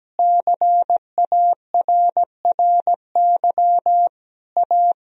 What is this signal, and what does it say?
Morse code, heard as a keyed tone, about 17 wpm, 700 Hz. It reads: CARRY A
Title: Sentences from Top 1000 Words - Review Part 2 - 17wpm